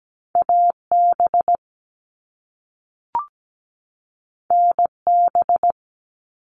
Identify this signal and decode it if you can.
Morse code, heard as a keyed tone, about 17 wpm, 700 Hz. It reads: AB NB
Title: Canadian Provinces and Territories 17wpm